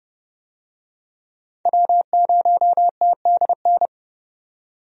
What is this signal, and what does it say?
Morse code, heard as a keyed tone, about 30 wpm, 700 Hz. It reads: W0TDN
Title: USA Call Signs - 5 Characters - 30wpm